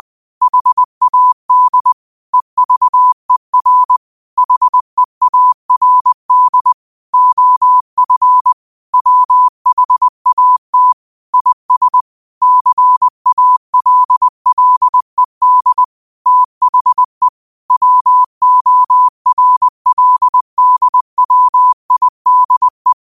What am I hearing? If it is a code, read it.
Morse code, heard as a keyed tone, about 20 wpm, 1000 Hz. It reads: HAD EVER HEARD OF WHAT IS CALLED THE WORLDWIDE